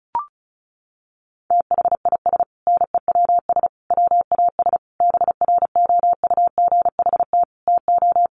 Morse code, heard as a keyed tone, about 34 wpm, 700 Hz. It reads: THIS NEWS WAS BROUGHT TO
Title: Sentences from Top 800 Words 35wpm